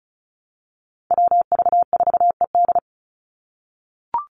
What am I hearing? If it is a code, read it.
Morse code, heard as a keyed tone, about 34 wpm, 700 Hz. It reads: WV4ED
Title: USA Call Signs 35wpm